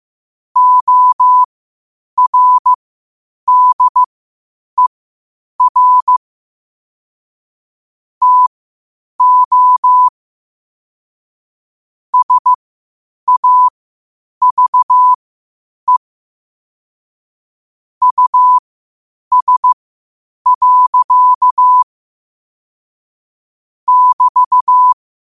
Morse code, heard as a keyed tone, about 15 wpm, spaced as for 8 wpm, 1000 Hz. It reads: ORDER TO SAVE US. <BT>